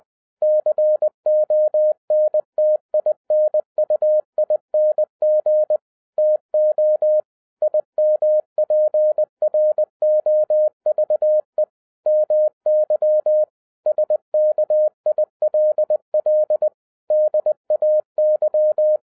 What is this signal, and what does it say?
Morse code, heard as a keyed tone, about 20 wpm, 610 Hz. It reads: CONTINUING TO IMPROVE MY SKILL DAY
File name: 2012-01-mp3 / n05-CAME BACK TO HAM RADIO AFTER 30 YEARS OF INTERVAL-20wpm